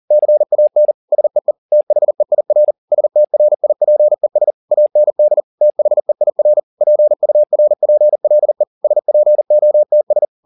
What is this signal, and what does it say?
Morse code, heard as a keyed tone, about 40 wpm, 600 Hz. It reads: CAN SEE THEIR STRIPES AND THEIR PURPLE SPOTS
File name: the old man and the sea 2 40wpm